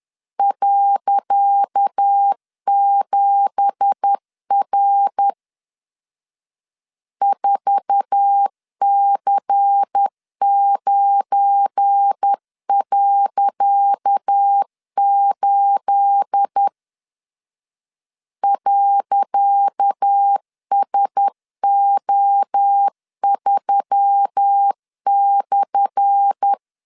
Morse code, heard as a keyed tone, about 10 wpm, 800 Hz. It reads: .7R 4C9.8 .SO3/
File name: LEZIONI / 10 SIGNS / 10C DOT 10wpm 2940